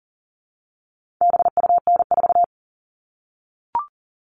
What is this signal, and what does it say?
Morse code, heard as a keyed tone, about 39 wpm, 700 Hz. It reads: BUN4
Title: All Cut Numbers - 4-Digit 40wpm